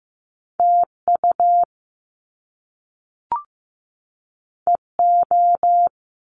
Morse code, heard as a keyed tone, about 15 wpm, 700 Hz. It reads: TU EO
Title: Top Two Letter Combinations in Words 15wpm